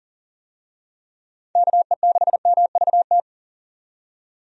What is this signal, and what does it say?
Morse code, heard as a keyed tone, about 40 wpm, 700 Hz. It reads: KE6MVT
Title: USA Call Signs - 6 Characters - 40wpm